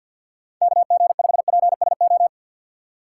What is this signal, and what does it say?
Morse code, heard as a keyed tone, about 52 wpm, 700 Hz. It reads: KG5PSO
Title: USA Call Signs - 3x Character Spacing - 50wpm